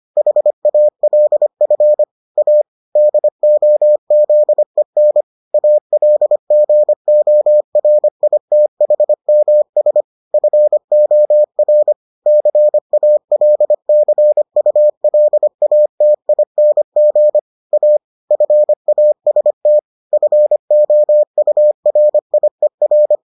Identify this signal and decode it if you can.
Morse code, heard as a keyed tone, about 25 wpm, 600 Hz. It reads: HALF A DOZEN ALGORITHMS FOR CALCULATING A FAST FOURIER